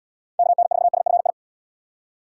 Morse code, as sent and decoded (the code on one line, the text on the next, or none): -.- -. ....- ... ..-. ..
KN4SFI